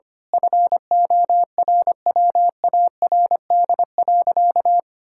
Morse code, heard as a keyed tone, about 25 wpm, 700 Hz. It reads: FORWARD.
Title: Pangram Challenge v2 - 25wpm